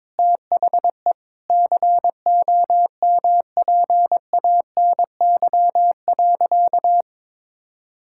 Morse code, heard as a keyed tone, about 22 wpm, 700 Hz. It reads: THE COMPANY.
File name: practice / 22wpm / Sentences with 3-Word Phrases - 22wpm